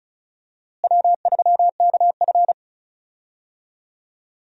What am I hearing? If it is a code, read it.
Morse code, heard as a keyed tone, about 34 wpm, 700 Hz. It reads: W3KF
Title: USA Call Signs - 4 Characters - Repeated 3x - 35wpm